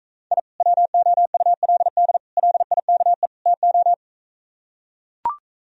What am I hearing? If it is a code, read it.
Morse code, heard as a keyed tone, about 45 wpm, 700 Hz. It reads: I WOULD LIKE TO